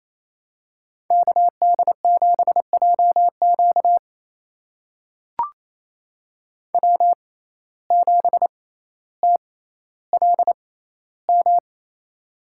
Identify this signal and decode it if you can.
Morse code, heard as a keyed tone, about 28 wpm, spaced as for 14 wpm, 700 Hz. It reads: KD7JQ W7TLM
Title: USA Call Signs - 5 Characters - 6x Character Spacing - 28wpm